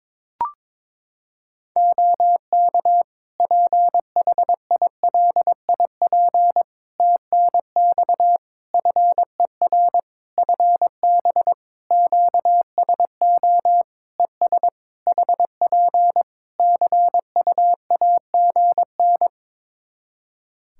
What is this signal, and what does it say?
Morse code, heard as a keyed tone, about 22 wpm, 700 Hz. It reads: OK PHILIP TNX FER FB QSO ES HP CUAGN
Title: K1USN SST QSO Practice 22wpm